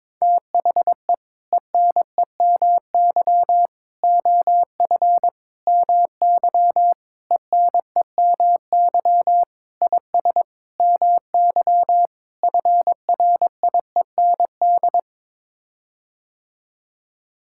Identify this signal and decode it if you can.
Morse code, heard as a keyed tone, about 22 wpm, 700 Hz. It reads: THE ENEMY OF MY ENEMY IS MY FRIEND
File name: practice / 22wpm / Sets of 8 Words - 22wpm